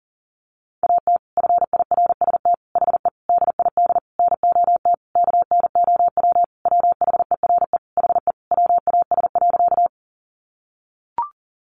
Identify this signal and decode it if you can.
Morse code, heard as a keyed tone, about 40 wpm, 700 Hz. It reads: AT FIRST HE DID NOT KNOW WHERE HE WAS.
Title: Sentences from Top 600 Words 40wpm